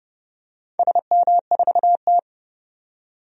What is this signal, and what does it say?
Morse code, heard as a keyed tone, about 29 wpm, 700 Hz. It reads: SM4T
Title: DXCC-100 Call Signs - 4 Characters - 30wpm